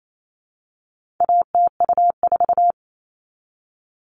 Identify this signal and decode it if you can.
Morse code, heard as a keyed tone, about 28 wpm, 700 Hz. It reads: ATU4
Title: All Cut Numbers - 4-Digit 28wpm